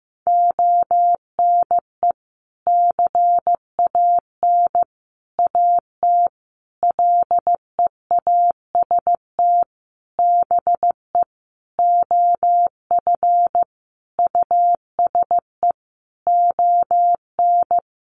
Morse code, heard as a keyed tone, about 15 wpm, 700 Hz. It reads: ONE CAN AT LEAST BE OF USE ON
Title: Sentences from Top 1200 Words 15wpm